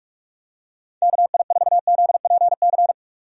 Morse code, heard as a keyed tone, about 46 wpm, 700 Hz. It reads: KI4ZPC
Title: USA Call Signs - 6 Characters - 45wpm